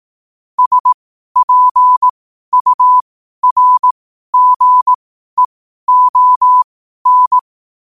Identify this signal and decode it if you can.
Morse code, heard as a keyed tone, about 18 wpm, spaced as for 12 wpm, 1000 Hz. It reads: SPURGEON
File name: qotdfiles / 12wpm